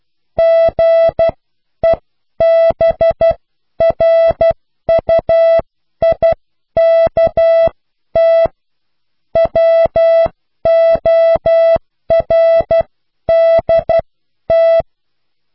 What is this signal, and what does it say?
Morse code, heard as a keyed tone, about 12 wpm, spaced as for 10 wpm, 660 Hz. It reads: GEBRUIKT WORDT